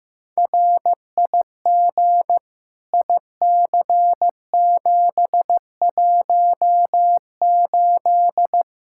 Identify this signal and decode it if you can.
Morse code, heard as a keyed tone, about 16 wpm, 700 Hz. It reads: RIG IC718
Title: K1USN SST QSO Practice 15wpm